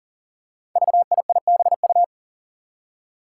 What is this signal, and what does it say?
Morse code, heard as a keyed tone, about 40 wpm, 700 Hz. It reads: VIIBU
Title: Five Challenging Letter-Number Combinations 40wpm